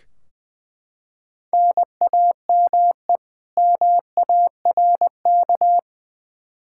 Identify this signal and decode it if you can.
Morse code, heard as a keyed tone, about 20 wpm, 700 Hz. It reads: NAME MARK
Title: Practice QSO - Exchange 1 - Beginning - 20wpm